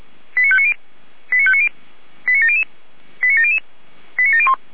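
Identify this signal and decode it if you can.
Five-tone selective call (ZVEI), heard as quick runs of five tones, tones of about 70 ms. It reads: 89590 89590 89700 89700 89701
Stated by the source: ZVEI five-tone selective call received via SDR, code 89280.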